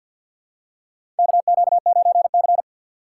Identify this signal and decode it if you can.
Morse code, heard as a keyed tone, about 53 wpm, 700 Hz. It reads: KQ9C
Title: USA Call Signs - 4 Characters - 2x Character Spacing - 50wpm